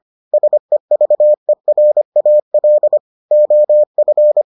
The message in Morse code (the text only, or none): SEVERAL OF